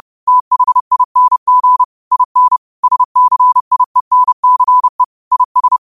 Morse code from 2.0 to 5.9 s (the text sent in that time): IN SCIENCE IS